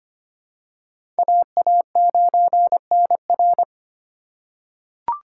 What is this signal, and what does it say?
Morse code, heard as a keyed tone, about 25 wpm, 700 Hz. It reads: AA9NR
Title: USA Call Signs - 5 Characters - 4x Character Spacing - 25wpm